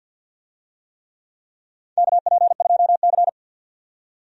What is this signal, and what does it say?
Morse code, heard as a keyed tone, about 50 wpm, 700 Hz. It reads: KP2C
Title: DXCC-100 Call Signs - 4 Characters - 50wpm